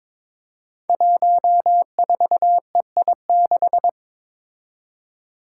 Morse code, heard as a keyed tone, about 22 wpm, 700 Hz. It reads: 14EI6
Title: Five Challenging Letter-Number Combinations 22wpm